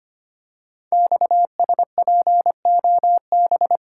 Morse code, heard as a keyed tone, about 25 wpm, 700 Hz. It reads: XSPOB
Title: Code Groups - 5 Letters - 25wpm